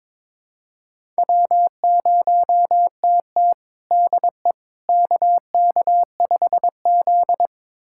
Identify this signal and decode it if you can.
Morse code, heard as a keyed tone, about 22 wpm, 700 Hz. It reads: W0TT DE KK5Z
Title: USA Callsign DE Callsign - 4 Characters - 22wpm